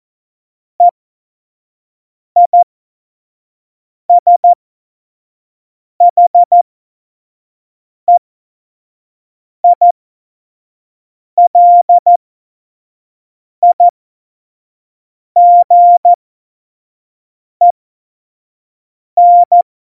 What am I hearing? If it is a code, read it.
Morse code, heard as a keyed tone, about 14 wpm, spaced as for 5 wpm, 700 Hz. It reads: EISHEILIGEN
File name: Morsekurs / Witze / Witz 23. April 2019, 25 F70